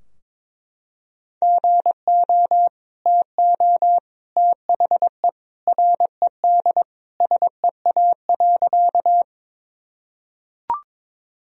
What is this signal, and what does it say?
Morse code, heard as a keyed tone, about 22 wpm, 700 Hz. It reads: GO TO THE RED SEA.
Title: Sentences from Top 1000 Words - Review Part 2 - 22wpm